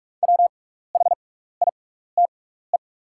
Morse code, heard as a keyed tone, about 43 wpm, spaced as for 15 wpm, 700 Hz. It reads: WHITE
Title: 6x Character Spacing - Top 300 Words - 45wpm